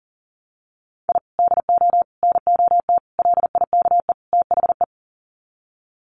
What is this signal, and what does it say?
Morse code, heard as a keyed tone, about 40 wpm, 700 Hz. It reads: I DO NOT LIKE THE
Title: Sets of 5 Words Encore 40wpm